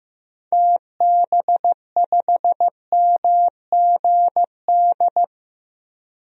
Morse code, heard as a keyed tone, about 15 wpm, 700 Hz. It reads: TB5MGD
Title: DXCC-100 Call Signs - 6 Characters - Repeated 2x - 15wpm